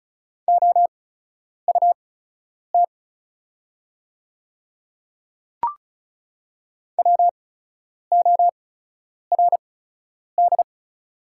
Morse code, heard as a keyed tone, about 34 wpm, spaced as for 10 wpm, 700 Hz. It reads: OUT WORD
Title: 8x Character Spacing - Top 100 Words - 35wpm